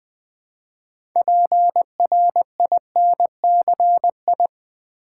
Morse code, heard as a keyed tone, about 20 wpm, 700 Hz. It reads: PRINCI
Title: Build A Word - 3 Letters At A Time - 9-Letter Words - 20wpm